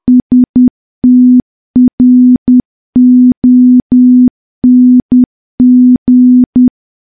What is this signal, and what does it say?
Morse code, heard as a keyed tone, about 10 wpm, 250 Hz. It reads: STRONG
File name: morse / morse-training-10WPM